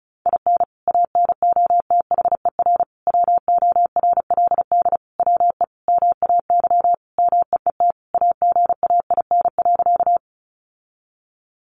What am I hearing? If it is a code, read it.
Morse code, heard as a keyed tone, about 34 wpm, 700 Hz. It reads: IN ANOTHER WORLD WE MAY MEET AGAIN.
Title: Sentences from Top 1100 Words 35wpm